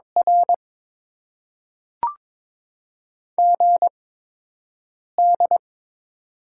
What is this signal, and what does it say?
Morse code, heard as a keyed tone, about 22 wpm, spaced as for 6 wpm, 700 Hz. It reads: R GD
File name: Top 90 QSO Elements - 8x Character Spacing - 22wpm